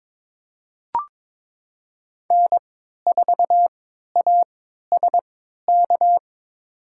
Morse code, heard as a keyed tone, about 22 wpm, spaced as for 13 wpm, 700 Hz. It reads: N4ASK
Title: USA Call Signs - 5 Characters - 3x Character Spacing - 22wpm